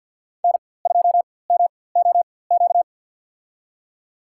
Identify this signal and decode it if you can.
Morse code, heard as a keyed tone, about 48 wpm, spaced as for 23 wpm, 700 Hz. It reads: N2MOQ
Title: USA Call Signs - 5 Characters - 4x Character Spacing - 50wpm